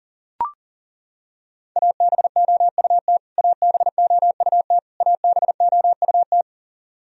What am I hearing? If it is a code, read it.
Morse code, heard as a keyed tone, about 40 wpm, 700 Hz. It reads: ABOUT ABOUT ABOUT
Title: Top 100 Words Repeated 40wpm